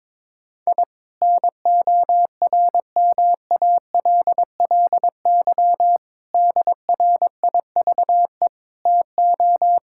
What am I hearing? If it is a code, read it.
Morse code, heard as a keyed tone, about 21 wpm, 700 Hz. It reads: I NORMALLY DRIVE TO
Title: Sentences from Top 1000 Words - Review Part 6 - 22wpm